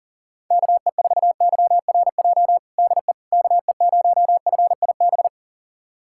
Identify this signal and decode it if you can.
Morse code, heard as a keyed tone, about 40 wpm, 700 Hz. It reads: KE4YRJ DE KE0FIB
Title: USA Callsign DE Callsign - 6 Characters - 40wpm